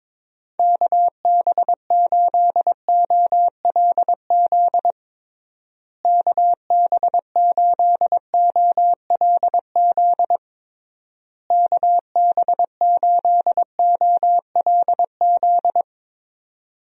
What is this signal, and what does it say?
Morse code, heard as a keyed tone, about 22 wpm, 700 Hz. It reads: KB8OLZ KB8OLZ KB8OLZ
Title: USA Call Signs - 6 Characters - Repeated 3x - 22wpm